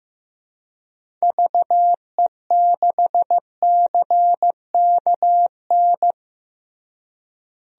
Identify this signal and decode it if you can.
Morse code, heard as a keyed tone, about 15 wpm, 700 Hz. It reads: VE6CKN
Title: DXCC-100 Call Signs - 6 Characters - Repeated 3x - 15wpm